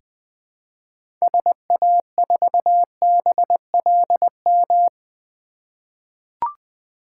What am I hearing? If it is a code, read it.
Morse code, heard as a keyed tone, about 21 wpm, 700 Hz. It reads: SA4BLM
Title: 2023 SOTA Chaser Call Signs - 20wpm